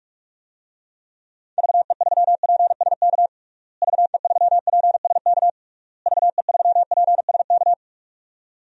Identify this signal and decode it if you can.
Morse code, heard as a keyed tone, about 46 wpm, 700 Hz. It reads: VE3PSK VE3PSK VE3PSK
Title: DXCC-100 Call Signs - 6 Characters - Repeated 3x - 45wpm